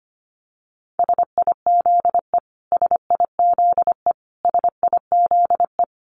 Morse code, heard as a keyed tone, about 25 wpm, 700 Hz. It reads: SIZE SIZE SIZE
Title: Top 500 Words Repeated 25wpm